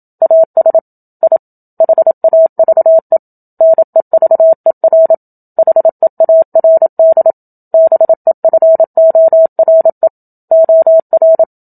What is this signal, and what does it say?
Morse code, heard as a keyed tone, about 27 wpm, 650 Hz. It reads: AS I HAVE NEVER HEARD BEFORE OR
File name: Book Two / 01-Chapter 1 - 27 WPM